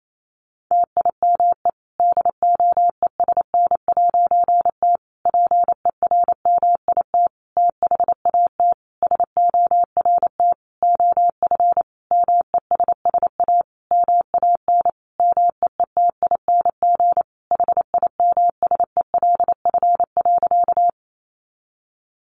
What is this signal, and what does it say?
Morse code, heard as a keyed tone, about 28 wpm, 700 Hz. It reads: TIME DOESN'T PERMIT THAT SORT OF MESS A MAN MEETING HIMSELF.
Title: A Sound Of Thunder 28wpm